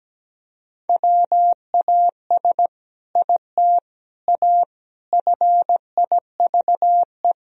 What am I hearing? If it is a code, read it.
Morse code, heard as a keyed tone, about 17 wpm, 700 Hz. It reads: WAS IT A FIVE
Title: Sentences from Top 1000 Words - Review Part 2 - 17wpm